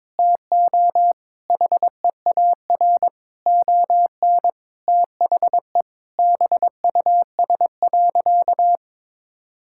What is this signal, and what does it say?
Morse code, heard as a keyed tone, about 22 wpm, 700 Hz. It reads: TO HEAR ON THE BUS.
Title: Sentences from Top 1000 Words - Review Part 6 - 22wpm